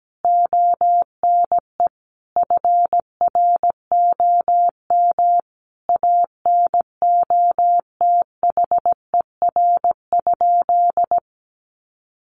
Morse code, heard as a keyed tone, about 17 wpm, 700 Hz. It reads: ONE FROM ANOTHER?
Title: Sentences from Top 1400 Words 17wpm